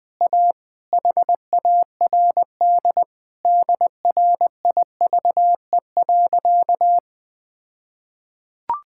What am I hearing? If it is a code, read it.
Morse code, heard as a keyed tone, about 20 wpm, 700 Hz. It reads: A HARD DRIVE.
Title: All-Age Appropriate Jokes - 20wpm